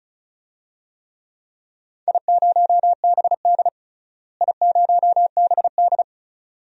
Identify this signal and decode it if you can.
Morse code, heard as a keyed tone, about 36 wpm, 700 Hz. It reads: I0BD I0BD
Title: DXCC-100 Call Signs - 4 Characters - Repeated 2x - 35wpm